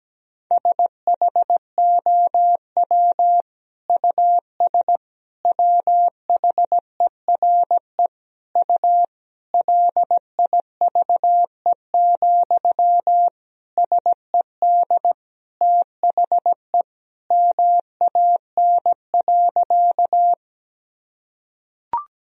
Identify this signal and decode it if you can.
Morse code, heard as a keyed tone, about 17 wpm, 700 Hz. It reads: SHOW US WHERE U LIVE, SED THE MAN.
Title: Sentences from Top 200 Words and Abbreviations 17wpm